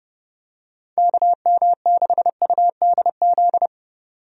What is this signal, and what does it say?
Morse code, heard as a keyed tone, about 30 wpm, 700 Hz. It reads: KM6UDZ
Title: USA Call Signs - 6 Characters - 30wpm